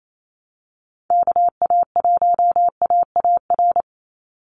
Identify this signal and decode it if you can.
Morse code, heard as a keyed tone, about 28 wpm, 700 Hz. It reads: KA1AAR
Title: USA Call Signs 28wpm